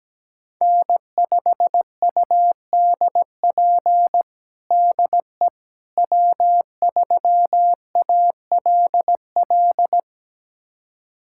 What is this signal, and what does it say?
Morse code, heard as a keyed tone, about 17 wpm, 700 Hz. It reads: N5UDP DE W3ALL
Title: USA Callsign DE Callsign - 5 Characters - 17wpm